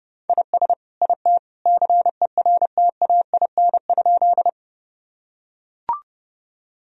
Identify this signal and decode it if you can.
Morse code, heard as a keyed tone, about 30 wpm, 700 Hz. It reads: IS IT CERTAIN?
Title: Sentences from Top 1000 Words - Review Part 2 - 30wpm